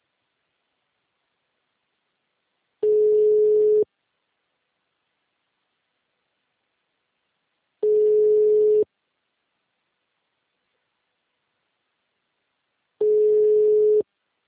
Ringback tone (Europe, 425 Hz).
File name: out- -2017- - - .14402